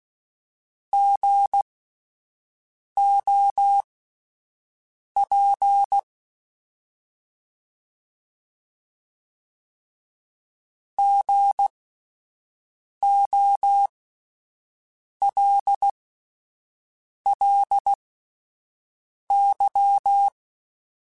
Morse code, heard as a keyed tone, about 16 wpm, spaced as for 5 wpm, 800 Hz. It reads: GOP GOLLY
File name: Spaced Alphas / Spaced Alphas mp3s / Gs sp @10